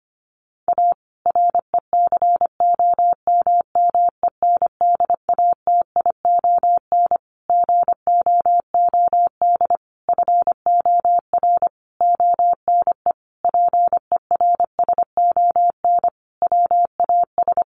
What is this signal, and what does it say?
Morse code, heard as a keyed tone, about 25 wpm, 700 Hz. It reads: A RECOMMENDATION GOOD FOR ONE PERSON WAS
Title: Sentences from Top 1400 Words 25wpm